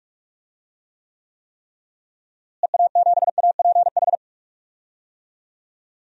Morse code, heard as a keyed tone, about 43 wpm, 700 Hz. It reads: EA7APH
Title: DXCC-100 Call Signs - 6 Characters - Repeated 3x - 45wpm